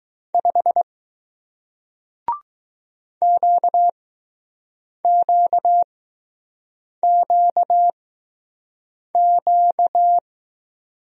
Morse code, heard as a keyed tone, about 21 wpm, spaced as for 6 wpm, 700 Hz. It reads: 5 QQQQ